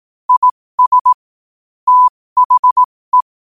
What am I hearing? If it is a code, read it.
Morse code, heard as a keyed tone, about 18 wpm, spaced as for 15 wpm, 1000 Hz. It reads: IS THE
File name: qotdfiles / 15wpm